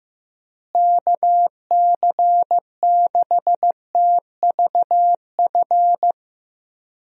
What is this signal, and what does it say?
Morse code, heard as a keyed tone, about 15 wpm, 700 Hz. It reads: KC6TVF
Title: USA Call Signs - 6 Characters - Repeated 2x - 15wpm